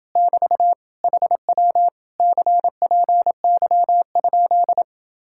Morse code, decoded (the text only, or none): <BT> HW CPY?